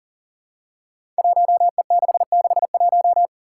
Morse code, heard as a keyed tone, about 42 wpm, 700 Hz. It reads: 1E661
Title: Five Challenging Letter-Number Combinations 40wpm